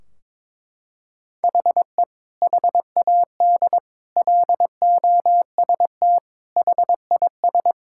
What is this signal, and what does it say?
Morse code, heard as a keyed tone, about 21 wpm, 700 Hz. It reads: HE HAD LOST HIS
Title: Sentences from Top 1600 Words 22wpm